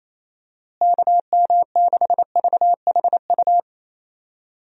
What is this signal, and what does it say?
Morse code, heard as a keyed tone, about 28 wpm, 700 Hz. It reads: KM6VHU
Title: USA Call Signs - 6 Characters - Repeated 3x - 28wpm